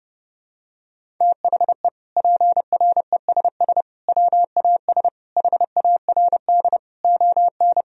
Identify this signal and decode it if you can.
Morse code, heard as a keyed tone, about 32 wpm, 700 Hz. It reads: THE PRESS WAS HARD ON
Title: Sentences from Top 1000 Words - Review Part 1 - 30wpm